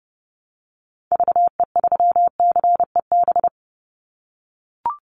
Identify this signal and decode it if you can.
Morse code, heard as a keyed tone, about 30 wpm, 700 Hz. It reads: VE3CEB
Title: Call Signs - 30wpm